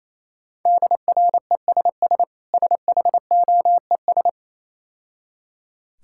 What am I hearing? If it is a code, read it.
Morse code, heard as a keyed tone, about 27 wpm, 690 Hz. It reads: DRESS SHOES
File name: Easy Words - Speed-Racing with Hints - 22wpm